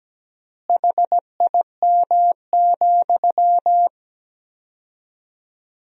Morse code, heard as a keyed tone, about 17 wpm, 700 Hz. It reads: HIM,
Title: 4x Word-Spacing - Sentences from Top 300 Words - 17wpm